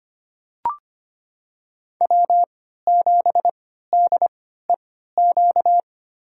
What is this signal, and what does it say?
Morse code, heard as a keyed tone, about 25 wpm, spaced as for 14 wpm, 700 Hz. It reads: W7DEQ
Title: USA Call Signs - 3x Character Spacing - 25wpm